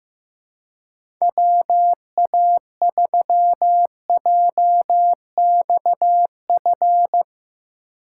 Morse code, heard as a keyed tone, about 15 wpm, 700 Hz. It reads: WA3JXF
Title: USA Call Signs - 6 Characters - 3x Character Spacing - 15wpm